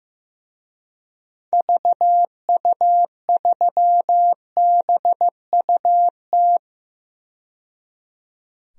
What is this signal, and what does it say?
Morse code, heard as a keyed tone, about 15 wpm, 700 Hz. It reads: VU3BUT
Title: DXCC-100 Call Signs - 6 Characters - 15wpm